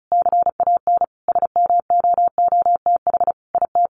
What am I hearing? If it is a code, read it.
Morse code, heard as a keyed tone, about 34 wpm, 700 Hz. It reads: CAN SMOOTH IT
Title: Sentences from Top 1400 Words 35wpm